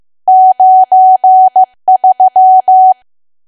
Morse code, heard as a keyed tone, about 15 wpm, 750 Hz. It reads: 93
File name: cwa / W1AW-Level2 / 15WPM